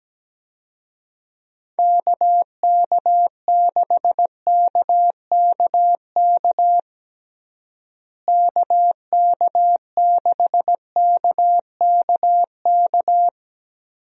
Morse code, heard as a keyed tone, about 17 wpm, 700 Hz. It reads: KK6KKK KK6KKK
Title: USA Call Signs - 6 Characters - Repeated 3x - 17wpm